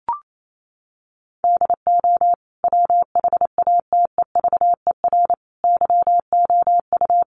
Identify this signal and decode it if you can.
Morse code, heard as a keyed tone, about 28 wpm, 700 Hz. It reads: DO WHATEVER YOU
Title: Sentences from Top 1100 Words 28wpm